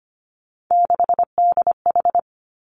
Morse code, heard as a keyed tone, about 25 wpm, 700 Hz. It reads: 6DH
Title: Three Challenging Letter-Number Combinations 25wpm